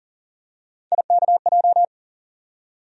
Morse code, heard as a keyed tone, about 40 wpm, 700 Hz. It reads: IKJ
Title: USA Call Sign Postfixes - 3 Letters 40wpm